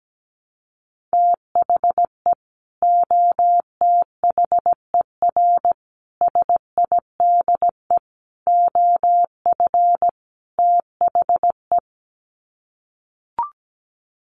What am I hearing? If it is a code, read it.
Morse code, heard as a keyed tone, about 17 wpm, 700 Hz. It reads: THE OTHER SIDE OF THE
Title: Sets of 5 Words 17wpm